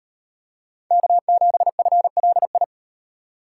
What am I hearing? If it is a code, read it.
Morse code, heard as a keyed tone, about 38 wpm, 700 Hz. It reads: K7FLI